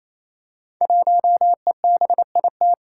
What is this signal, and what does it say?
Morse code, heard as a keyed tone, about 27 wpm, 700 Hz. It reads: 1EBIT
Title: Five Challenging Letter-Number Combinations 28wpm